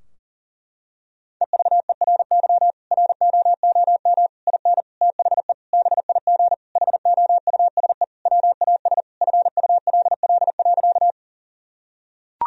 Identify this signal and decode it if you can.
Morse code, heard as a keyed tone, about 40 wpm, 700 Hz. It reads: EVERY ROOM IN THE BIG HOUSE WAS FULL.
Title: Sentences from Top 1000 Words - Review Part 3 - 40wpm